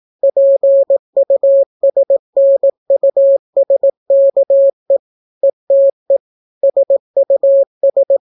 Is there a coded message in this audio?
Yes: Morse code from 0.2 to 8.2 s, heard as a keyed tone, about 18 wpm, 550 Hz. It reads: PUSNUSKE ETE SUS